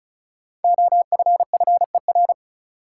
Morse code, heard as a keyed tone, about 36 wpm, 700 Hz. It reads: OFFER
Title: Five Letter Words - 35wpm